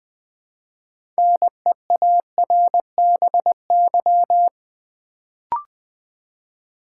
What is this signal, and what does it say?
Morse code, heard as a keyed tone, about 20 wpm, 700 Hz. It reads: NEARBY
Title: Build A Word - 2 Letters At A Time - 6-Letter Words - 20wpm